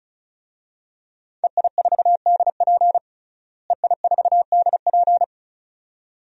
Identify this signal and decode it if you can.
Morse code, heard as a keyed tone, about 36 wpm, 700 Hz. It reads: EI4DP EI4DP
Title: DXCC-100 Call Signs - 5 Characters - Repeated 2x - 35wpm